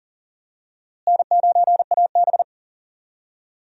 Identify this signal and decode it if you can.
Morse code, heard as a keyed tone, about 40 wpm, 700 Hz. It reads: N9AB
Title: USA Call Signs - 4 Characters - Repeated 2x - 40wpm